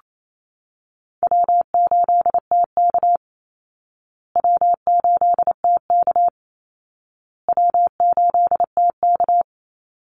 Morse code, heard as a keyed tone, about 28 wpm, 700 Hz. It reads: W8TK W8TK W8TK
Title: Easy Call Signs Repeated 3x - 28wpm